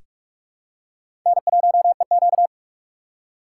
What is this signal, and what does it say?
Morse code, heard as a keyed tone, about 48 wpm, 700 Hz. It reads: N1EQ
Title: USA Call Signs - 4 Characters - 8x Character Spacing - 45wpm